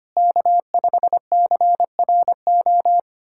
Morse code, heard as a keyed tone, about 25 wpm, 700 Hz. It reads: K5CRO
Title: USA Callsign DE Callsign 25wpm